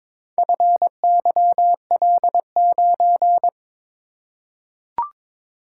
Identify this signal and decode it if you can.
Morse code, heard as a keyed tone, about 22 wpm, 700 Hz. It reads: FYL9
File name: practice / 22wpm / Four Letter-Number Combinations - 22wpm